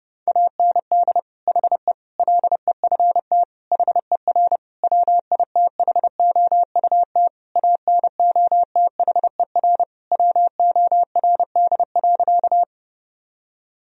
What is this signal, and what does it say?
Morse code, heard as a keyed tone, about 30 wpm, 700 Hz. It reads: AND HE LEFT HER WITHOUT ANOTHER WORD.
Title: Sentences from Top 1000 Words - Review Part 3 - 30wpm